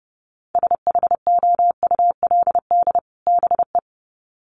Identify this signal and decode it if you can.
Morse code, heard as a keyed tone, about 30 wpm, 700 Hz. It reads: SHOULD BE
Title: Sets of 2 Words 30wpm